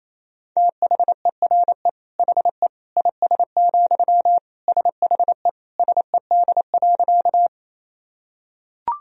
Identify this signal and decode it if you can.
Morse code, heard as a keyed tone, about 28 wpm, 700 Hz. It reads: THERE HE IS, SHE SED.
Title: Sentences from Top 100 Words and Abbreviations 28wpm